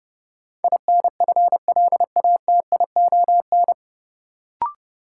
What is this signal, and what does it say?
Morse code, heard as a keyed tone, about 32 wpm, 700 Hz. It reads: INFLATION
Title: Sentences from Top 1100 Words - New Word Drill Alt - 30wpm